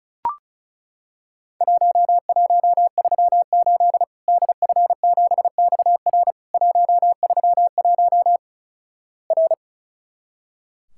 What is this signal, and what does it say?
Morse code, heard as a keyed tone, about 36 wpm, 700 Hz. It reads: 1138 DF7XR 131 R
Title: Worked All Europe DX Contest - QTC Exchange - 35wpm